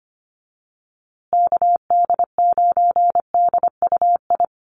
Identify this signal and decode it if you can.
Morse code, heard as a keyed tone, about 25 wpm, 700 Hz. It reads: KD9DUI
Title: USA Call Signs Repeated 3x 25wpm